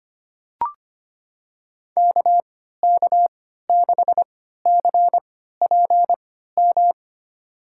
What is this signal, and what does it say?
Morse code, heard as a keyed tone, about 25 wpm, spaced as for 14 wpm, 700 Hz. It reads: KK6CPM
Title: USA Call Signs - 3x Character Spacing - 25wpm